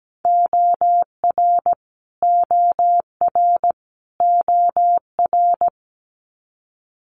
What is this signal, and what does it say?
Morse code, heard as a keyed tone, about 17 wpm, 700 Hz. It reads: OR OR OR
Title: Multi-letter Phonemes Repeated 3x 17wpm